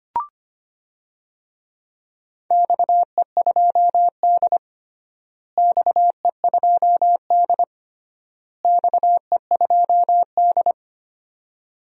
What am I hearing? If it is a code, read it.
Morse code, heard as a keyed tone, about 25 wpm, 700 Hz. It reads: XE2D XE2D XE2D
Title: DXCC-100 Call Signs - 4 Characters - Repeated 3x - 25wpm